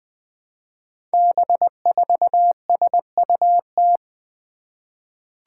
Morse code, heard as a keyed tone, about 20 wpm, 700 Hz. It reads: B4SUT
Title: Five Challenging Letter-Number Combinations 20wpm